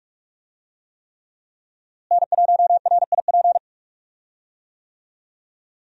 Morse code, heard as a keyed tone, about 47 wpm, 700 Hz. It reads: N1RIP
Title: USA Call Signs - 5 Characters - Repeated 3x - 45wpm